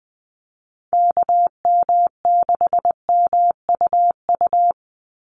Morse code, heard as a keyed tone, about 20 wpm, 700 Hz. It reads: KM6MUU
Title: USA Call Signs 20wpm